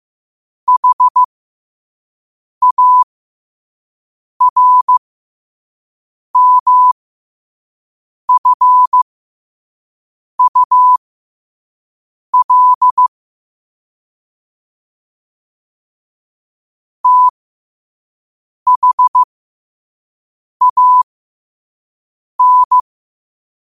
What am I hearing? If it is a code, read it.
Morse code, heard as a keyed tone, about 15 wpm, spaced as for 5 wpm, 1000 Hz. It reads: HARMFUL THAN
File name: qotdfiles / 5wpm